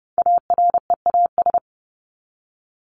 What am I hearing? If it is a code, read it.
Morse code, heard as a keyed tone, about 30 wpm, 700 Hz. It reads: AREAS
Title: ES:S Suffix - Top 100 Words with Base Words 30wpm